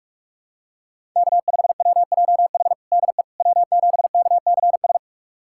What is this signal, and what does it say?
Morse code, heard as a keyed tone, about 46 wpm, 700 Hz. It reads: K5WJH DE W7KCS
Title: USA Callsign DE Callsign - 5 Characters - 45wpm